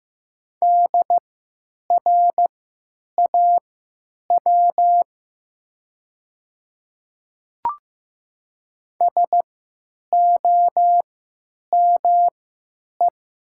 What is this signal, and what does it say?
Morse code, heard as a keyed tone, about 15 wpm, spaced as for 9 wpm, 700 Hz. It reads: DRAW SOME
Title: 3x Character Spacing - Top 300 Words - 15wpm